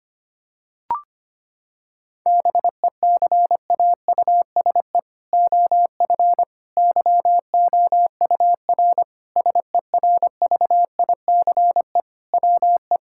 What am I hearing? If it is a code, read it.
Morse code, heard as a keyed tone, about 25 wpm, 700 Hz. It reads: BECAUSE OF YOUR SERVICE WE
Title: Sentences from Top 1000 Words - Review Part 5 - 25wpm